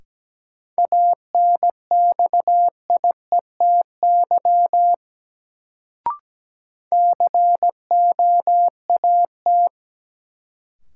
Morse code, heard as a keyed tone, about 17 wpm, 700 Hz. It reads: ANXIETY COAT
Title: Sentences from Top 1600 Words - New Word Drill - 17wpm